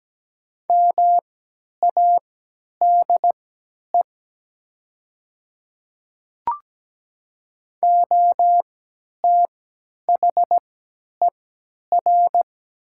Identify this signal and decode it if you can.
Morse code, heard as a keyed tone, about 17 wpm, spaced as for 10 wpm, 700 Hz. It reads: MADE OTHER
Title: 3x Character Spacing - Top 200 Words - 17wpm